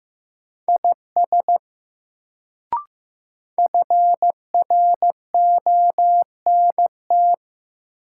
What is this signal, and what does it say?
Morse code, heard as a keyed tone, about 15 wpm, 700 Hz. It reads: IS FRONT
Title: Sets of 2 Words - New Word Drill - 15wpm